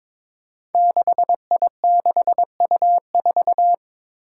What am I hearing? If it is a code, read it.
Morse code, heard as a keyed tone, about 21 wpm, 700 Hz. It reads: 6I6U4
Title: Five Challenging Letter-Number Combinations 22wpm